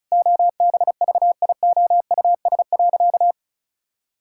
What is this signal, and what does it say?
Morse code, heard as a keyed tone, about 36 wpm, 700 Hz. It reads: OBVIOUS.
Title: Sentences from Top 1600 Words 35wpm